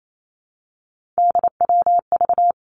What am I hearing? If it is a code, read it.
Morse code, heard as a keyed tone, about 28 wpm, 700 Hz. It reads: DWV
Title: Three Challenging Letter-Number Combinations 28wpm